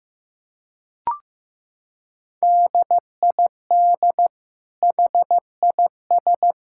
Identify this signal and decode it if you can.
Morse code, heard as a keyed tone, about 15 wpm, 700 Hz. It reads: DID HIS
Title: Sentences from Top 1000 Words - Review Part 5 - 15wpm